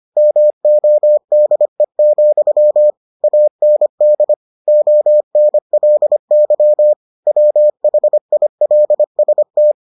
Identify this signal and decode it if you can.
Morse code, heard as a keyed tone, about 25 wpm, 600 Hz. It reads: MODE, AND ONLY WHILST